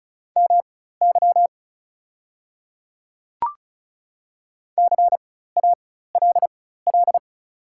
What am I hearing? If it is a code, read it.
Morse code, heard as a keyed tone, about 36 wpm, spaced as for 16 wpm, 690 Hz. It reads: MY CALL
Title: 4x Character Spacing - Top 100 Words - 35wpm